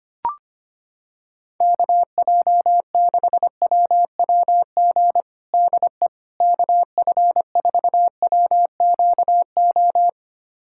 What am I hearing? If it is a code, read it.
Morse code, heard as a keyed tone, about 25 wpm, 700 Hz. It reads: KJ6WWG DE KF4WQO
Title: USA Callsign DE Callsign - 6 Characters - 25wpm